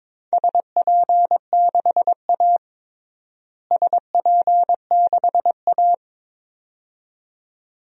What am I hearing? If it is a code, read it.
Morse code, heard as a keyed tone, about 22 wpm, 700 Hz. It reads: SP6A SP6A
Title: DXCC-100 Call Signs - 4 Characters - Repeated 2x - 22wpm